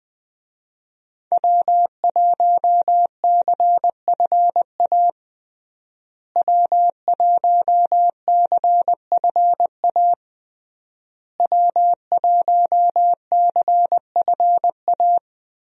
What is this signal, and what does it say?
Morse code, heard as a keyed tone, about 20 wpm, 700 Hz. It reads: W1CFA W1CFA W1CFA
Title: USA Call Signs - 5 Characters - Repeated 3x - 20wpm